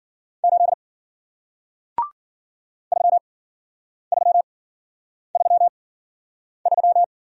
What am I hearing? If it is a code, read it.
Morse code, heard as a keyed tone, about 56 wpm, spaced as for 9 wpm, 700 Hz. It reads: 8 3333